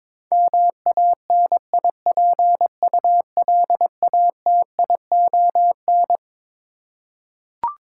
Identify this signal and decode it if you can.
Morse code, heard as a keyed tone, about 21 wpm, 700 Hz. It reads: MANIPULATION
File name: Twelve Letter Words - 22wpm